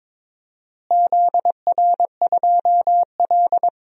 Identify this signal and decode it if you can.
Morse code, heard as a keyed tone, about 21 wpm, 700 Hz. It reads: ZR2L
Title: DXCC-100 Call Signs - 4 Characters - Repeated 3x - 22wpm